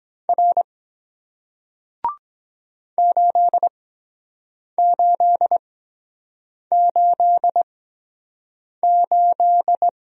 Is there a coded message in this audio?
Yes: Morse code from 0.3 to 9.9 s, heard as a keyed tone, about 23 wpm, spaced as for 7 wpm, 700 Hz. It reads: R 8888